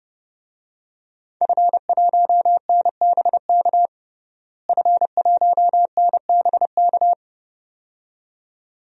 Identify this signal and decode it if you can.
Morse code, heard as a keyed tone, about 30 wpm, 700 Hz. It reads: F1NBK F1NBK
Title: DXCC-100 Call Signs - 5 Characters - Repeated 2x - 30wpm